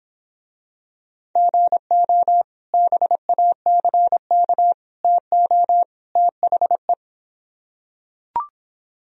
Morse code, heard as a keyed tone, about 26 wpm, 700 Hz. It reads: GO BACK TO THE